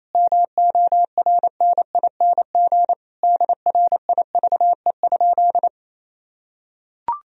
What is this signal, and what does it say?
Morse code, heard as a keyed tone, about 28 wpm, 700 Hz. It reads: MORNING DRIVE?
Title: Sentences from Top 1000 Words - Review Part 3 - 28wpm